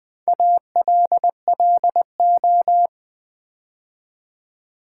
Morse code, heard as a keyed tone, about 20 wpm, 700 Hz. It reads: ALLO
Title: Build A Word - 4 Letters At A Time - 10-Letter Words - 20wpm